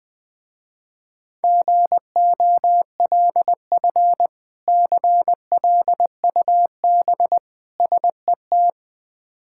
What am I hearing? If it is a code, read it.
Morse code, heard as a keyed tone, about 20 wpm, 700 Hz. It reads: GOLF CLUB SET
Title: Difficult Words with Hints - 20wpm